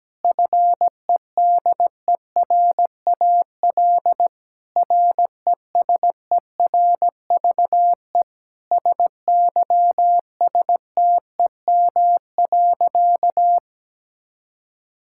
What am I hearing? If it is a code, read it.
Morse code, heard as a keyed tone, about 17 wpm, 700 Hz. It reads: FEDERAL RESERVE SYSTEM.
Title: Sentences from Top 1000 Words - Review Part 6 - 17wpm